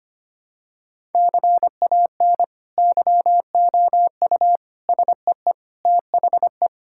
Morse code, heard as a keyed tone, about 25 wpm, 700 Hz. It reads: CAN YOU SEE THE
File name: practice / 25wpm / Sentences with 2-Word Phrases - 25wpm